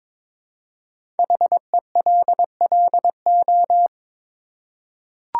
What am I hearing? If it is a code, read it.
Morse code, heard as a keyed tone, about 22 wpm, 700 Hz. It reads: HELLO
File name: 5 Letter Words - First 3 Letters Given - 22wpm